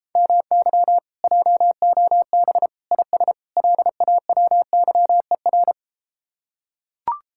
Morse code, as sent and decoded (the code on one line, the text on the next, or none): -- -.-- / .--- --- -... / .. ... / .-.. .- .-- -.-- . .-.
MY JOB IS LAWYER